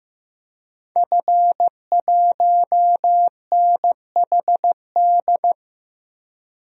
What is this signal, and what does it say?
Morse code, heard as a keyed tone, about 15 wpm, 700 Hz. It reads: F1NHD
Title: DXCC-100 Call Signs - 5 Characters - Repeated 2x - 15wpm